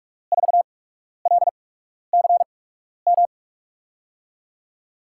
Morse code, heard as a keyed tone, about 46 wpm, spaced as for 12 wpm, 700 Hz. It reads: 4LCM
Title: USA Call Signs - 6 Characters - 8x Character Spacing - 45wpm